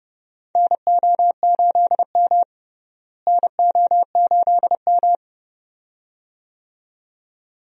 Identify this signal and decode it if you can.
Morse code, heard as a keyed tone, about 30 wpm, 700 Hz. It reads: NO8M NO8M
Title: USA Call Signs - 4 Characters - Repeated 2x - 30wpm